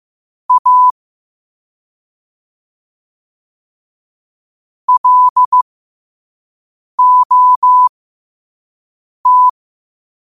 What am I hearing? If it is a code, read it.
Morse code, heard as a keyed tone, about 15 wpm, spaced as for 5 wpm, 1000 Hz. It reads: A LOT
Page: New quotes every day in morse code at 5 Words per minute.